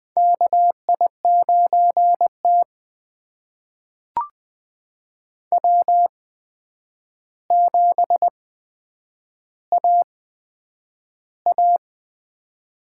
Morse code, heard as a keyed tone, about 20 wpm, spaced as for 9 wpm, 700 Hz. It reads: KI9T W7AA
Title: USA Call Signs - 4 Characters - 8x Character Spacing - 20wpm